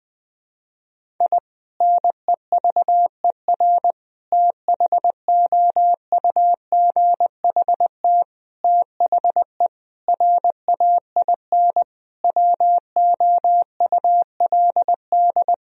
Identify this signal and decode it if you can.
Morse code, heard as a keyed tone, about 20 wpm, 700 Hz. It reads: I NEVER THOUGHT THE RAIN WOULD
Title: Sentences from Top 1000 Words - Review Part 1 - 20wpm